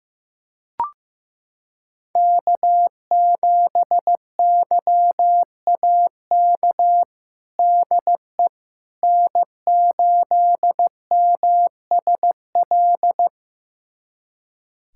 Morse code, heard as a keyed tone, about 15 wpm, 700 Hz. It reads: K7YAK DE N8MSL
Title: USA Callsign DE Callsign - 5 Characters - 15wpm